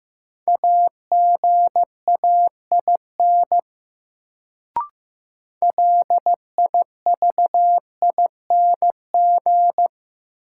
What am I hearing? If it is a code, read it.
Morse code, heard as a keyed tone, about 15 wpm, 700 Hz. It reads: AGAIN LIVING
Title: Sets of 4 Words - New Word Drill - 15wpm